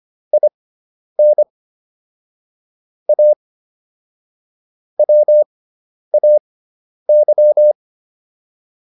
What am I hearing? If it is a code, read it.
Morse code, heard as a keyed tone, about 25 wpm, spaced as for 10 wpm, 600 Hz. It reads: IN A WAY